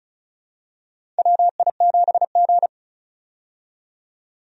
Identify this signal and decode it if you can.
Morse code, heard as a keyed tone, about 36 wpm, 700 Hz. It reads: WI7G
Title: USA Call Signs - 4 Characters - 35wpm